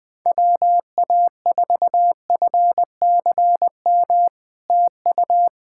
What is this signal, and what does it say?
Morse code, heard as a keyed tone, about 20 wpm, 700 Hz. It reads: WA4FCM TU
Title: SOTA QSO Practice 20wpm